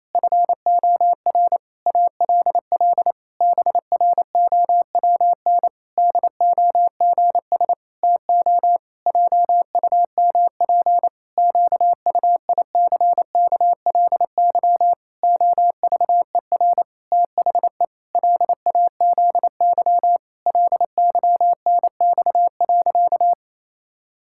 Morse code, heard as a keyed tone, about 28 wpm, 700 Hz. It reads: FOR ALL BROWN DOGS TO JUMP QUICKLY OVER THE LAZY LYNX.
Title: Pangram Challenge v2 - 28wpm